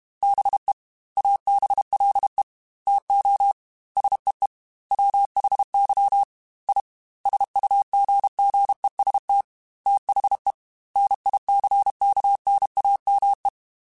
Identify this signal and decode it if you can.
Morse code, heard as a keyed tone, about 32 wpm, 800 Hz. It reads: BE ABLE TO SEE WHY I SUGGEST THE NICKNAME
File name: Technical Subjects / Technical Subjects-mp3s / OR Gate @30